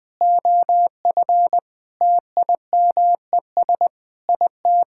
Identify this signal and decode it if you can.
Morse code, heard as a keyed tone, about 20 wpm, 700 Hz. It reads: OF TIMES IT
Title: Sets of 8 Words - 20wpm